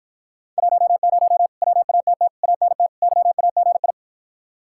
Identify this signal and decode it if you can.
Morse code, heard as a keyed tone, about 52 wpm, 700 Hz. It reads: 10 WATT ANT YAGI